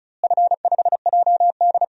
Morse code, heard as a keyed tone, about 36 wpm, 700 Hz. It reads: F5JD
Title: DXCC-100 Call Signs - 4 Characters - Repeated 3x - 35wpm